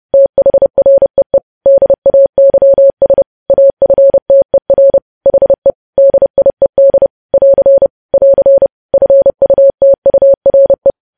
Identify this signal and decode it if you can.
Morse code, heard as a keyed tone, about 30 wpm, 550 Hz. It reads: THREE DAYS AFTER HE DIED <AR> <AR> FUTURE
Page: news headlines in morse code at 30 wpm